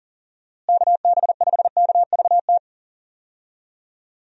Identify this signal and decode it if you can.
Morse code, heard as a keyed tone, about 40 wpm, 700 Hz. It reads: KB5KVT
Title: USA Call Signs - 6 Characters - Repeated 3x - 40wpm